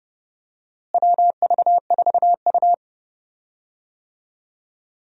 Morse code, heard as a keyed tone, about 30 wpm, 700 Hz. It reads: WV4U
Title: USA Call Signs - 4 Characters - Repeated 2x - 30wpm